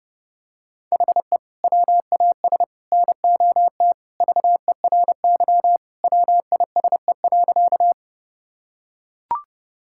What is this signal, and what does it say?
Morse code, heard as a keyed tone, about 30 wpm, 700 Hz. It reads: HE WAS NOT VERY WISE.
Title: Sentences from Top 1600 Words 30wpm